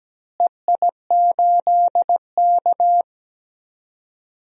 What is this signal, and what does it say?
Morse code, heard as a keyed tone, about 17 wpm, 700 Hz. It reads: EI8K
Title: DXCC-100 Call Signs - 4 Characters - Repeated 3x - 17wpm